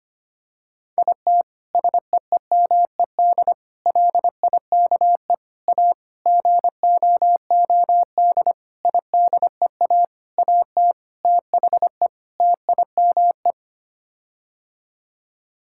Morse code, heard as a keyed tone, about 25 wpm, 700 Hz. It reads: IT SEEMED LIKE A GOOD IDEA AT THE TIME
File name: Sets of 9 Words - 25wpm